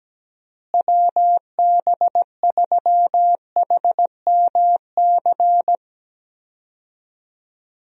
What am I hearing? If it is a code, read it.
Morse code, heard as a keyed tone, about 17 wpm, 700 Hz. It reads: WB3HMC
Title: USA Call Signs - 6 Characters - Repeated 2x - 17wpm